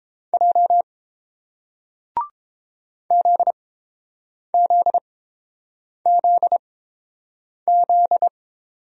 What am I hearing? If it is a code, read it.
Morse code, heard as a keyed tone, about 29 wpm, spaced as for 7 wpm, 700 Hz. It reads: J ZZZZ